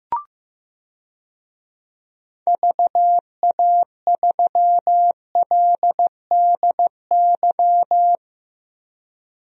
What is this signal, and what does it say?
Morse code, heard as a keyed tone, about 16 wpm, 700 Hz. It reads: VA3LDY
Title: DXCC-100 Call Signs - 6 Characters - 15wpm